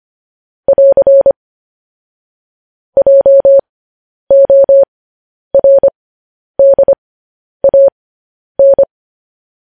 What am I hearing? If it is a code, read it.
Morse code, heard as a keyed tone, about 25 wpm, spaced as for 10 wpm, 550 Hz. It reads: <AR> JORDAN
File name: podcast / News headlines-1x-25x10